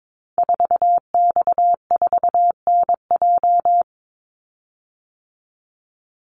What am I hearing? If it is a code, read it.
Morse code, heard as a keyed tone, about 22 wpm, 700 Hz. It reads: 4X4NJ
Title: Call Signs - 22wpm